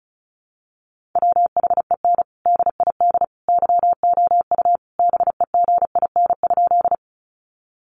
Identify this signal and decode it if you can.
Morse code, heard as a keyed tone, about 34 wpm, 700 Hz. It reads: WHEN DID YOU BEGIN?
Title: Sentences from Top 300 Words 35wpm